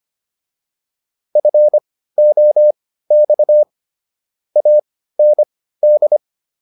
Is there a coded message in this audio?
Yes: Morse code from 1.4 to 6.2 s, heard as a keyed tone, about 25 wpm, spaced as for 15 wpm, 600 Hz. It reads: FOX AND